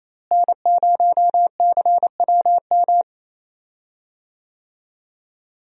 Morse code, heard as a keyed tone, about 27 wpm, 700 Hz. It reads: N0CWM
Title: USA Call Signs - 5 Characters - Repeated 3x - 28wpm